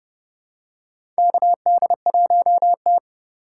Morse code, heard as a keyed tone, about 30 wpm, 700 Hz. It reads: KD1T
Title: USA Call Signs - 4 Characters - 4x Character Spacing - 30wpm